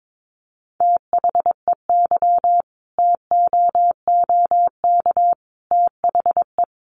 Morse code, heard as a keyed tone, about 22 wpm, 700 Hz. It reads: THEY TOOK THE
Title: Sentences from Top 1200 Words 22wpm